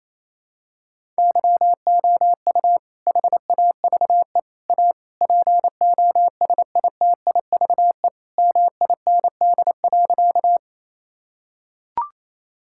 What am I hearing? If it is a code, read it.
Morse code, heard as a keyed tone, about 27 wpm, 700 Hz. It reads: YOU HAVE A POSITIVE MIND.
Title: Sentences from Top 1000 Words - Review Part 5 - 28wpm